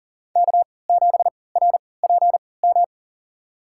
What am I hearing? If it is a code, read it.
Morse code, heard as a keyed tone, about 40 wpm, spaced as for 23 wpm, 700 Hz. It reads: K7RPM
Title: USA Call Signs - 5 Characters - 3x Character Spacing - 40wpm